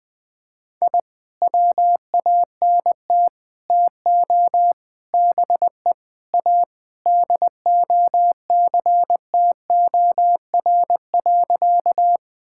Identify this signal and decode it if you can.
Morse code, heard as a keyed tone, about 20 wpm, 700 Hz. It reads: I WANT TO BE A DOCTOR.
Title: Sentences with 3-Word Phrases - 20wpm